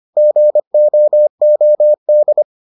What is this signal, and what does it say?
Morse code, heard as a keyed tone, about 25 wpm, 600 Hz. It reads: GOOD